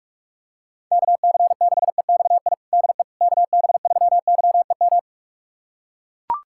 Morse code, heard as a keyed tone, about 42 wpm, 700 Hz. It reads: KC6EXI DE KB3YEM
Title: USA Callsign DE Callsign 45wpm